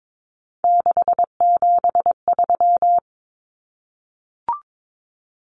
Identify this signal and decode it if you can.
Morse code, heard as a keyed tone, about 22 wpm, 700 Hz. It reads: 673
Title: Common Cut Numbers - 3-Digit 22wpm